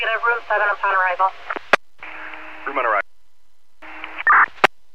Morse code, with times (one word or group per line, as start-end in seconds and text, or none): none